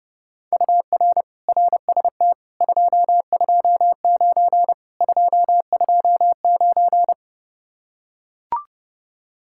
Morse code, heard as a keyed tone, about 29 wpm, 700 Hz. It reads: UR RST 229 229